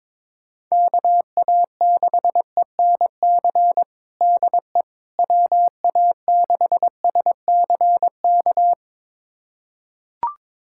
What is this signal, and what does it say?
Morse code, heard as a keyed tone, about 22 wpm, 700 Hz. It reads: KA6ENC DE WA6SCK
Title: USA Callsign DE Callsign - 6 Characters - 22wpm